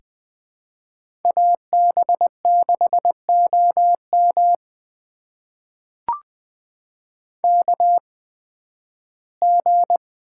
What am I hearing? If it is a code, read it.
Morse code, heard as a keyed tone, about 20 wpm, spaced as for 13 wpm, 700 Hz. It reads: AB6OM KG
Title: USA Call Signs - 8x Character Spacing - 20wpm